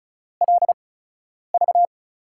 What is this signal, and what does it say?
Morse code, heard as a keyed tone, about 36 wpm, spaced as for 10 wpm, 700 Hz. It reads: LV
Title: USA Call Signs - 8x Character Spacing - 35wpm